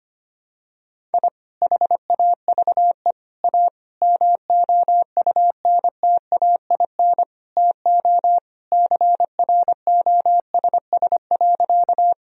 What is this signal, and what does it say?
Morse code, heard as a keyed tone, about 25 wpm, 700 Hz. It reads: I HAVE A MOUNTAIN TO CROSS.
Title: Sentences from Top 1000 Words - Review Part 1 - 25wpm